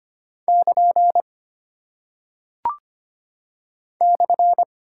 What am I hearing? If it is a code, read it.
Morse code, heard as a keyed tone, about 25 wpm, 700 Hz. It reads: <KN> /
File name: Single Punctuation-Prosign - 25wpm